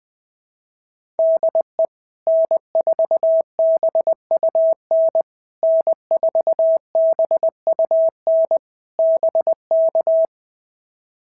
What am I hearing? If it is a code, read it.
Morse code, heard as a keyed tone, about 20 wpm, 650 Hz. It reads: DE N4BUN N4BUN BK
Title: POTA QSO Practice - 20wpm